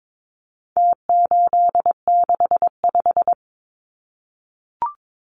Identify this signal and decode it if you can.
Morse code, heard as a keyed tone, about 22 wpm, 700 Hz. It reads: T865
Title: Numbers - Contest Style - 4-Digit 22wpm